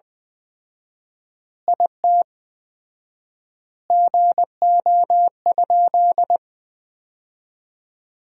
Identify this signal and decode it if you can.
Morse code, heard as a keyed tone, about 20 wpm, 700 Hz. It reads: IT GO?
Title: 4x Word-Spacing - Sentences from Top 200 Words - 20wpm